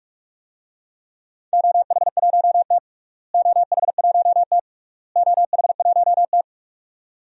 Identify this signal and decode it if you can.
Morse code, heard as a keyed tone, about 46 wpm, 690 Hz. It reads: OH1T OH1T OH1T
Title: DXCC-100 Call Signs - 4 Characters - Repeated 3x - 45wpm